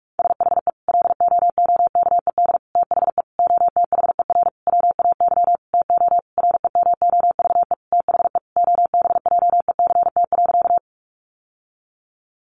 Morse code, heard as a keyed tone, about 44 wpm, 700 Hz. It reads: SHE LOOKED THE OTHER WAY TO REMOVE THE OBJECT.
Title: Sentences from Top 800 Words 45wpm